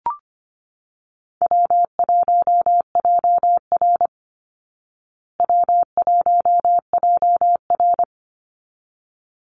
Morse code, heard as a keyed tone, about 25 wpm, 700 Hz. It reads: W1JR W1JR
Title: Easy Call Signs Repeated 2x - 25wpm